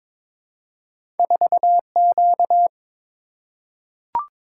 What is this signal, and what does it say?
Morse code, heard as a keyed tone, about 22 wpm, 700 Hz. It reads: 4Q
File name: practice / 22wpm / Two Letter-Number Combinations - 22wpm